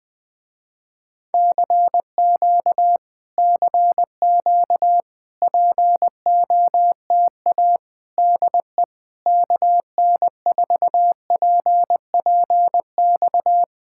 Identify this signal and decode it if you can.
Morse code, heard as a keyed tone, about 20 wpm, 700 Hz. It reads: CQ CQ POTA DE KN4PPX
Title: Speed-Racing Common QSO Phrases - 20wpm